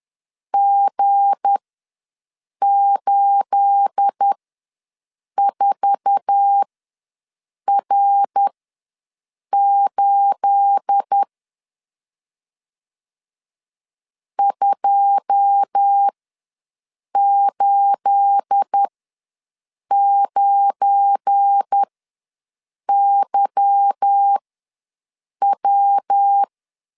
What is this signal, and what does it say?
Morse code, heard as a keyed tone, about 10 wpm, spaced as for 6 wpm, 800 Hz. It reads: G84R8 289YW
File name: LEZIONI / 9 28 ws 10wpm 4124